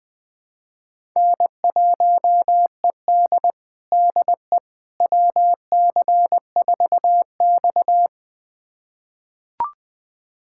Morse code, heard as a keyed tone, about 20 wpm, 700 Hz. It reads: N1ED DE WC4X
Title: USA Callsign DE Callsign - 4 Characters - 20wpm